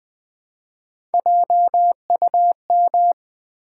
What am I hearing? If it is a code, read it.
Morse code, heard as a keyed tone, about 20 wpm, 700 Hz. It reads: JUM
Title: Build A Word - 1 Letter At A Time - 3 to 5-Letter Words - 20wpm